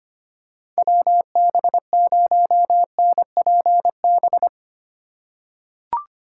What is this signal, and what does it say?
Morse code, heard as a keyed tone, about 25 wpm, 700 Hz. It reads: WB0NPB
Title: USA Call Signs - 4x Character Spacing - 25wpm